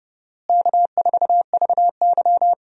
Morse code, heard as a keyed tone, about 30 wpm, 700 Hz. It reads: K4VY
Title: USA Call Signs - 4 Characters - Repeated 2x - 30wpm